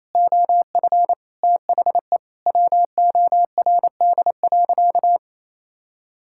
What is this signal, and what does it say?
Morse code, heard as a keyed tone, about 28 wpm, 700 Hz. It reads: OF THE WORD.
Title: Sentences from Top 1000 Words - Review Part 6 - 28wpm